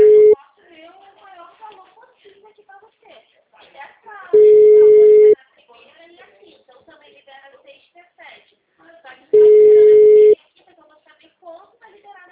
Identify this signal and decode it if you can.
Ringback tone (Europe, 425 Hz), with speech.
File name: out- -2034- - - .6943